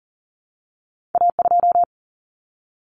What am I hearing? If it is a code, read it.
Morse code, heard as a keyed tone, about 41 wpm, 700 Hz. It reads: A2
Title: Common Cut Numbers - 2-Digit 40wpm